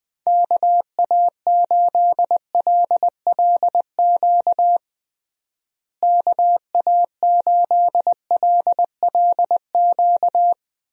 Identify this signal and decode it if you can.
Morse code, heard as a keyed tone, about 20 wpm, 700 Hz. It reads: KA8LLQ KA8LLQ
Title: USA Call Signs - 6 Characters - Repeated 3x - 20wpm